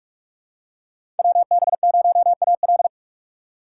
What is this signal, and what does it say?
Morse code, heard as a keyed tone, about 46 wpm, 700 Hz. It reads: WB0AL
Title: USA Call Signs - 5 Characters - 6x Character Spacing - 45wpm